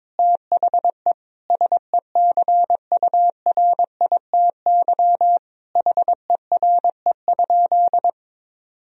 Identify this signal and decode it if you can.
Morse code, heard as a keyed tone, about 22 wpm, 700 Hz. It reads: THE SECURITY HERE?
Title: Sentences from Top 1000 Words - Review Part 4 - 22wpm